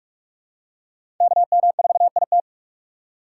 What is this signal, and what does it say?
Morse code, heard as a keyed tone, about 42 wpm, 700 Hz. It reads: KM4IT
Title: USA Call Signs - 6x Character Spacing - 45wpm